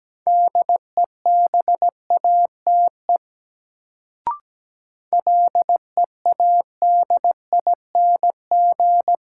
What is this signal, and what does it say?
Morse code, heard as a keyed tone, about 17 wpm, 700 Hz. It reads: DEBATE LEADING
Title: Sentences from Top 1100 Words - New Word Drill - 17wpm